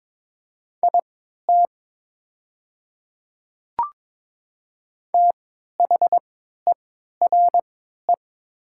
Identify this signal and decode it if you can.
Morse code, heard as a keyed tone, about 22 wpm, spaced as for 13 wpm, 700 Hz. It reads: IT THERE
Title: 3x Character Spacing - Top 100 Words - 22wpm